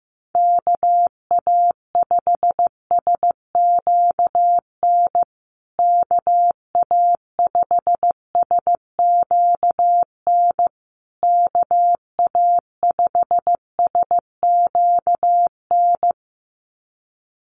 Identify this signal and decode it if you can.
Morse code, heard as a keyed tone, about 14 wpm, 700 Hz. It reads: KA5SQN KA5SQN KA5SQN
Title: USA Call Signs Repeated 3x 15wpm